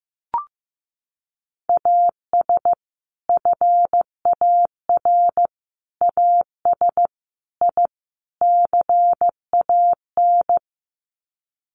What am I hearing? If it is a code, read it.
Morse code, heard as a keyed tone, about 15 wpm, 700 Hz. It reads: AS FAR AS I CAN
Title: Sets of 5 Words Encore 15wpm